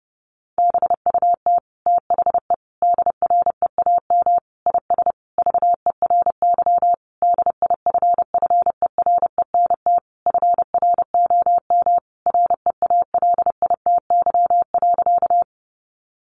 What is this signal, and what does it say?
Morse code, heard as a keyed tone, about 30 wpm, 700 Hz. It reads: BUT THE DREAM IS VERY DIFFERENT FROM REALITY.
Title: Sentences from Top 1500 Words 30wpm